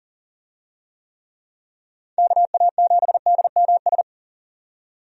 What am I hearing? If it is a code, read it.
Morse code, heard as a keyed tone, about 40 wpm, 700 Hz. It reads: KA7DMS
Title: USA Call Signs - 6 Characters - 40wpm